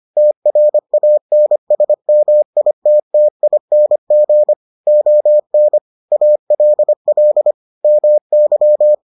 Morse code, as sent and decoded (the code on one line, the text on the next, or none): - .-. .- -. ... -- .. - - .. -. --. / --- -. / .- .-.. .-.. / -- -.--
TRANSMITTING ON ALL MY